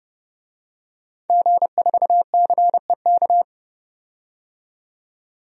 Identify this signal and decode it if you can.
Morse code, heard as a keyed tone, about 30 wpm, 700 Hz. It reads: G4CEK
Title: DXCC-100 Call Signs - 5 Characters - 30wpm